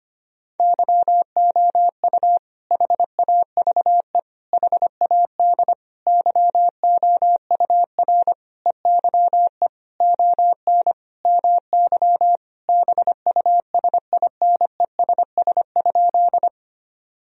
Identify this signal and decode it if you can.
Morse code, heard as a keyed tone, about 25 wpm, 700 Hz. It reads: YOU HAVE HAD YOUR EYE ON MY BUSINESS?
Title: Sentences from Top 1000 Words - Review Part 1 - 25wpm